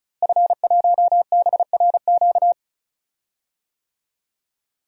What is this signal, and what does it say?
Morse code, heard as a keyed tone, about 36 wpm, 700 Hz. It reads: F1BRQ
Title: DXCC-100 Call Signs - 5 Characters - Repeated 3x - 35wpm